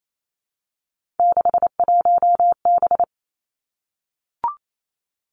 Morse code, heard as a keyed tone, about 28 wpm, 700 Hz. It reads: 61B
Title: Three Challenging Letter-Number Combinations 28wpm